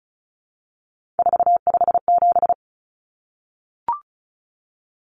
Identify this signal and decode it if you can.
Morse code, heard as a keyed tone, about 34 wpm, 700 Hz. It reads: 457
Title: Numbers - 3-Digit 35wpm